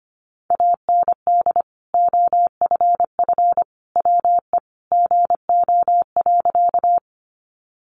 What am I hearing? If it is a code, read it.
Morse code, heard as a keyed tone, about 25 wpm, 700 Hz. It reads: AND OFF WE GO.
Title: Sentences from Top 200 Words 25wpm